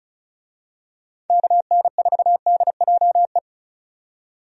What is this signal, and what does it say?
Morse code, heard as a keyed tone, about 36 wpm, 700 Hz. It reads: KN4DJE
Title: USA Call Signs - 6 Characters - 35wpm